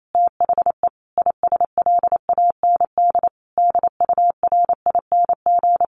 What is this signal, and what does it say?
Morse code, heard as a keyed tone, about 28 wpm, 700 Hz. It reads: THE ISLAND DURING
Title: Sentences from Top 1400 Words 28wpm